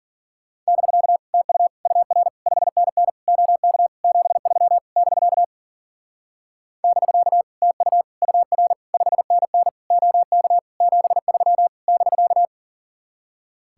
Morse code, heard as a keyed tone, about 50 wpm, 700 Hz. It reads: <BK> TU UR 5NN OK 73 <BK> <BK> TU UR 5NN OK 73 <BK>
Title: Speed-Racing Common QSO Phrases - 40wpm